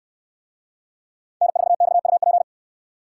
Morse code, heard as a keyed tone, about 67 wpm, 700 Hz. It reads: N4XUP